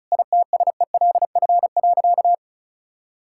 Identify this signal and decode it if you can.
Morse code, heard as a keyed tone, about 36 wpm, 700 Hz. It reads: ITSELF.
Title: Sentences from Top 1000 Words - Review Part 5 - 35wpm